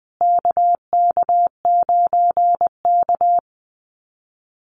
Morse code, heard as a keyed tone, about 20 wpm, 700 Hz. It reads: KK9K
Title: USA Call Signs Repeated 3x 20wpm